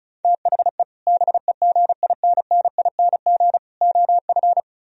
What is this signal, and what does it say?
Morse code, heard as a keyed tone, about 43 wpm, 700 Hz. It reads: THE BEGINNING OF
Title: Speed-Racing - Sets of 3 Words 35wpm